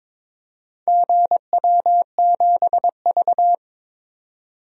Morse code, heard as a keyed tone, about 23 wpm, 700 Hz. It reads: GW7V
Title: DXCC-100 Call Signs - 4 Characters - 22wpm